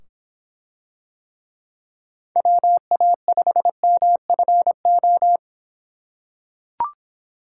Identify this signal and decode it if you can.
Morse code, heard as a keyed tone, about 25 wpm, 700 Hz. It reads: WA5MFO